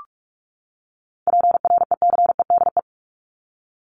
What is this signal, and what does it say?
Morse code, heard as a keyed tone, about 44 wpm, 700 Hz. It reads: PRECEDE
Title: ING Suffix - Top 100 Words with Base Words 45wpm